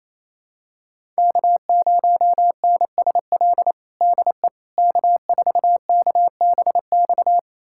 Morse code, heard as a keyed tone, about 28 wpm, 700 Hz. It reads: K0NSL DE K4KBX
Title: USA Callsign DE Callsign - 5 Characters - 28wpm